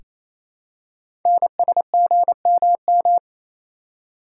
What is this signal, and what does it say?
Morse code, heard as a keyed tone, about 27 wpm, 700 Hz. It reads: NSGMM
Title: Code Groups - 5 Letters - 28wpm